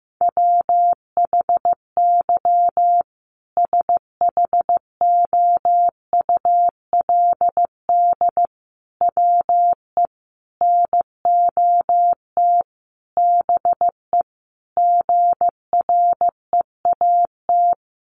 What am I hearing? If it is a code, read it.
Morse code, heard as a keyed tone, about 15 wpm, 700 Hz. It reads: WHY SHOULD WE NOT BE GREAT
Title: Sentences from Top 1500 Words 15wpm